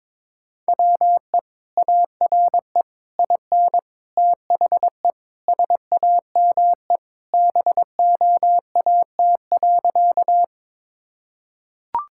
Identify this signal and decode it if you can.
Morse code, heard as a keyed tone, about 22 wpm, 700 Hz. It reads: WE ARE IN THE SAME BOAT.
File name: practice / 22wpm / Sentences with 3-Word Phrases - 22wpm